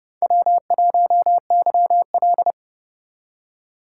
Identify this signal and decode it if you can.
Morse code, heard as a keyed tone, about 31 wpm, 700 Hz. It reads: W1YL
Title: USA Call Signs - 4 Characters - 4x Character Spacing - 30wpm